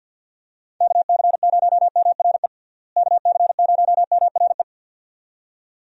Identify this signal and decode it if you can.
Morse code, heard as a keyed tone, about 52 wpm, 700 Hz. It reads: KC0MRE KC0MRE
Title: USA Call Signs - 6 Characters - Repeated 2x - 50wpm